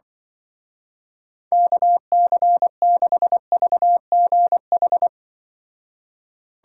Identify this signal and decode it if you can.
Morse code, heard as a keyed tone, about 25 wpm, 700 Hz. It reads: KC6VGH
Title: Speed Racing - USA Call Signs 20wpm